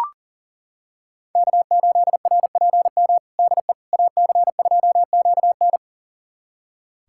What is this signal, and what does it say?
Morse code, heard as a keyed tone, about 40 wpm, 700 Hz. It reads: K8RPM DE AC2QN
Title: USA Callsign DE Callsign - 5 Characters - 40wpm